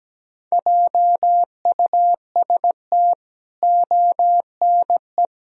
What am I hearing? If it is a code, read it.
Morse code, heard as a keyed tone, about 17 wpm, 700 Hz. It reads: JUST ONE
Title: Sentences from Top 1000 Words - Review Part 1 - 17wpm